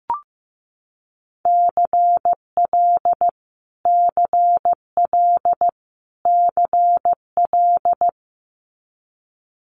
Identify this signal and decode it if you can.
Morse code, heard as a keyed tone, about 15 wpm, 700 Hz. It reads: CL CL CL
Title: Multi-letter Phonemes Repeated 3x 15wpm